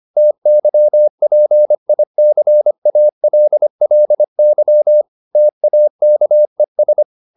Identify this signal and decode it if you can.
Morse code, heard as a keyed tone, about 25 wpm, 600 Hz. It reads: TYPICALLY TAKES
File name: SS211 25